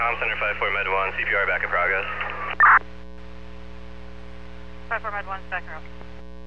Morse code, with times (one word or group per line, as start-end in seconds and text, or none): none